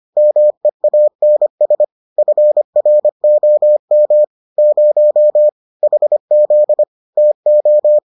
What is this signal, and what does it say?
Morse code, heard as a keyed tone, about 25 wpm, 600 Hz. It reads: MEANS FROM 0 HZ TO